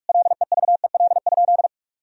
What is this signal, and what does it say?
Morse code, heard as a keyed tone, about 47 wpm, 700 Hz. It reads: LEVEL?
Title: Sentences from Top 1000 Words - Review Part 4 - 45wpm